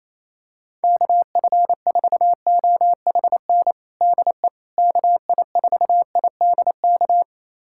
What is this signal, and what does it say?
Morse code, heard as a keyed tone, about 27 wpm, 700 Hz. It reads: KF4OHN DE KI4IDK
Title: USA Callsign DE Callsign - 6 Characters - 28wpm